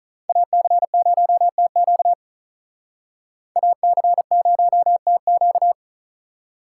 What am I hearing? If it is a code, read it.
Morse code, heard as a keyed tone, about 46 wpm, 700 Hz. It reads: AC0TQ AC0TQ